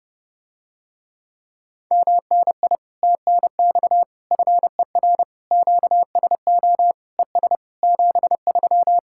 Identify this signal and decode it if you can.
Morse code, heard as a keyed tone, about 30 wpm, 700 Hz. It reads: MNI TNX FER QSO ES 73